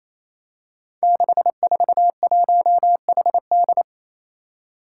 Morse code, heard as a keyed tone, about 28 wpm, 700 Hz. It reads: 641HD
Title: Five Challenging Letter-Number Combinations 28wpm